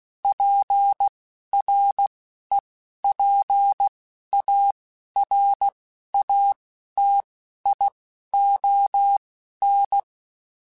Morse code, heard as a keyed tone, about 16 wpm, spaced as for 12 wpm, 800 Hz. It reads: PREPARATION
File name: Quotes / Quotes-mp3s / Persperation @10